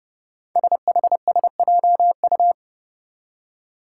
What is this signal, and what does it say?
Morse code, heard as a keyed tone, about 30 wpm, 700 Hz. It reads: SHSJU
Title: Five Challenging Letter-Number Combinations 30wpm